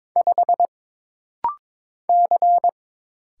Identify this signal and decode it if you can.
Morse code, heard as a keyed tone, about 22 wpm, 700 Hz. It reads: 5 C
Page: This rapid-fire timing is geared to push your ICR (Instant Character Recognition) skills.